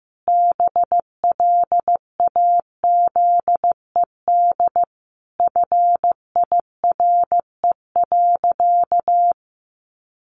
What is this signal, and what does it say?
Morse code, heard as a keyed tone, about 15 wpm, 700 Hz. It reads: BLAZED FIRE.
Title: A Sound Of Thunder 15wpm